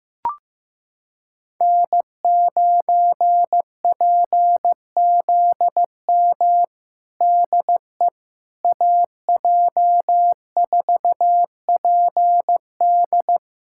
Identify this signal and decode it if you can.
Morse code, heard as a keyed tone, about 16 wpm, 700 Hz. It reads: N9PZM DE AJ4PD
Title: USA Callsign DE Callsign - 5 Characters - 15wpm